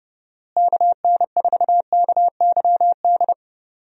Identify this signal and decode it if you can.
Morse code, heard as a keyed tone, about 30 wpm, 700 Hz. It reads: KN4KYD
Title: USA Call Signs - 6 Characters - Repeated 2x - 30wpm